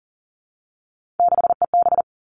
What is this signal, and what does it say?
Morse code, heard as a keyed tone, about 41 wpm, 700 Hz. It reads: 6EB
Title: Three Challenging Letter-Number Combinations 40wpm